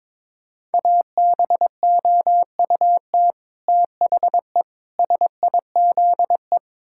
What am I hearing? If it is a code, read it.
Morse code, heard as a keyed tone, about 22 wpm, 700 Hz. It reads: ABOUT THE SIZE
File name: practice / 22wpm / Build A Phrase - 5-Word Phrases - 22wpm